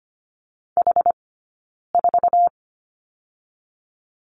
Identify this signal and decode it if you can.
Morse code, heard as a keyed tone, about 25 wpm, 700 Hz. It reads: H 4
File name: Lesson 15d - Number 5 - Two Characters 25wpm